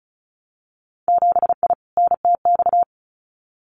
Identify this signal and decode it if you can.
Morse code, heard as a keyed tone, about 36 wpm, 700 Hz. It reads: 7I NTX
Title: Winter Field Day Exchanges Encore 35wpm